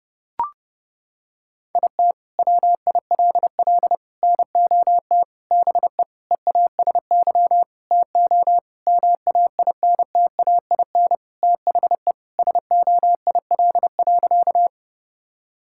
Morse code, heard as a keyed tone, about 30 wpm, 700 Hz. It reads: IT WILL NOT BE EASY TO MAINTAIN THE SOIL.
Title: Sentences from Top 1000 Words - Review Part 5 - 30wpm